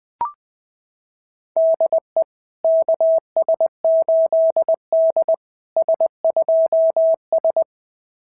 Morse code, heard as a keyed tone, about 20 wpm, 660 Hz. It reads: DE KS8D S2S
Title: SOTA QSO Practice 20wpm